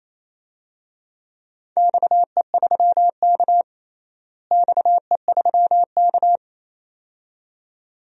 Morse code, heard as a keyed tone, about 28 wpm, 700 Hz. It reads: XE3K XE3K
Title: DXCC-100 Call Signs - 4 Characters - Repeated 2x - 28wpm